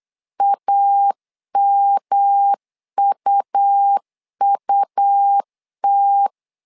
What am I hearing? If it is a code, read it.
Morse code, heard as a keyed tone, about 8 wpm, 800 Hz. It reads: AMUUT
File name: LEZIONI / 3 MWRO / 3 M 8wpm 2300